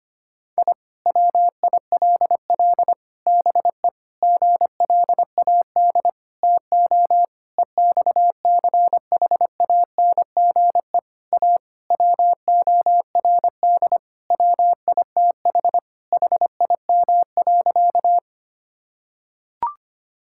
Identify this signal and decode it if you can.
Morse code, heard as a keyed tone, about 25 wpm, 700 Hz. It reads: I WILL BE GLAD TO EXCHANGE A WORD WITH HIM.
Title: Sentences from Top 1000 Words - Review Part 6 - 25wpm